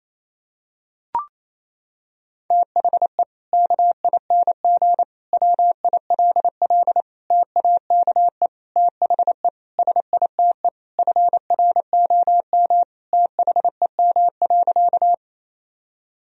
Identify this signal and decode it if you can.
Morse code, heard as a keyed tone, about 28 wpm, 700 Hz. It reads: THE KING WILL TAKE THE SITE FROM THEM.
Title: Sentences from Top 1000 Words - Review Part 4 - 28wpm